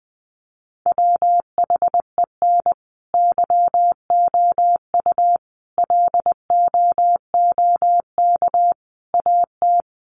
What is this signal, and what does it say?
Morse code, heard as a keyed tone, about 20 wpm, 700 Hz. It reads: WHEN YOU LOOK AT
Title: Sets of 5 Words 20wpm